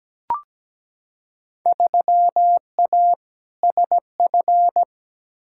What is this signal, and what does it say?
Morse code, heard as a keyed tone, about 17 wpm, 700 Hz. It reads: 3A SF
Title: Field Day Exchanges Encore - 17wpm